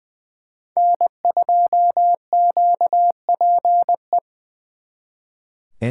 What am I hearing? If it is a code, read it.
Morse code, heard as a keyed tone, about 23 wpm, 700 Hz. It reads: N2QPE
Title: Speed Racing - USA Call Signs 20wpm